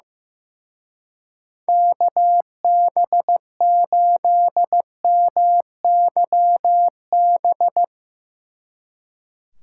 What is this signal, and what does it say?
Morse code, heard as a keyed tone, about 17 wpm, 700 Hz. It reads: KB8MYB
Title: Speed Racing - USA Call Signs 15wpm